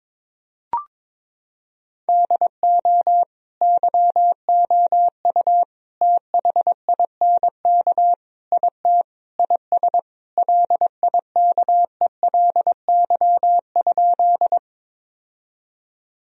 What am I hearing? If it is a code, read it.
Morse code, heard as a keyed tone, about 22 wpm, 700 Hz. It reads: DO YOU THINK IT IS LIKELY?
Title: Sentences from Top 1000 Words - Review Part 4 - 22wpm